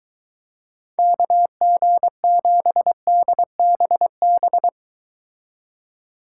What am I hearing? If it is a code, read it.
Morse code, heard as a keyed tone, about 23 wpm, 700 Hz. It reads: KG7DBB